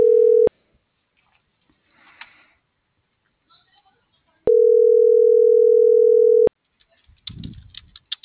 Ringback tone (North America), 440 + 480 Hz.